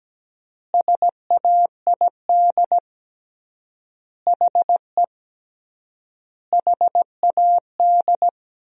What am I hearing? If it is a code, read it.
Morse code, heard as a keyed tone, about 17 wpm, 700 Hz. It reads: SAID HE HAD
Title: 3x Word-Spacing - Sentences from Top 500 Words - 17wpm